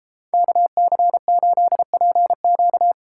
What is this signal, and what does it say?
Morse code, heard as a keyed tone, about 31 wpm, 700 Hz. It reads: KC8PQ